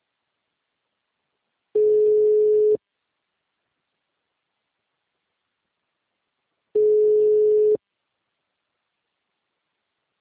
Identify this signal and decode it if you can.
Ringback tone (Europe, 425 Hz).